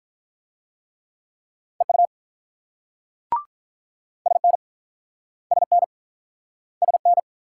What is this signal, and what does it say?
Morse code, heard as a keyed tone, about 48 wpm, 700 Hz. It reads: EU SN SN SN